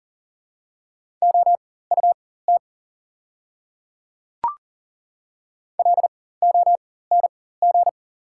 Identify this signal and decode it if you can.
Morse code, heard as a keyed tone, about 40 wpm, spaced as for 19 wpm, 690 Hz. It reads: OUT LONG
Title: 4x Character Spacing - Top 100 Words - 40wpm